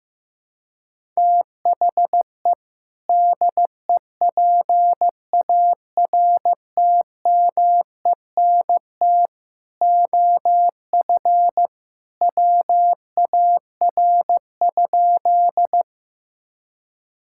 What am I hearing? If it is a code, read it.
Morse code, heard as a keyed tone, about 15 wpm, 700 Hz. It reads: THE DEPARTMENT OF WAR?
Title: Sentences from Top 1000 Words - Review Part 5 - 15wpm